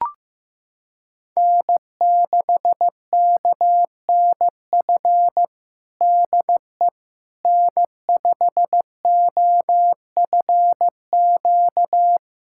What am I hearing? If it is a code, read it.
Morse code, heard as a keyed tone, about 15 wpm, 700 Hz. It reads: N6KNF DE N5OFQ
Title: USA Callsign DE Callsign - 5 Characters - 15wpm